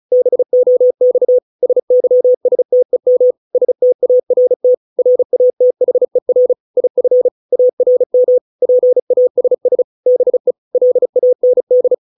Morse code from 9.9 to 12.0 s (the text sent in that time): BE LAND